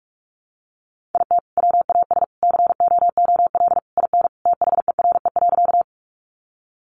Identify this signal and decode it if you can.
Morse code, heard as a keyed tone, about 44 wpm, 700 Hz. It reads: IT WAS COOL IN THERE.
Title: Sentences from Top 1200 Words 45wpm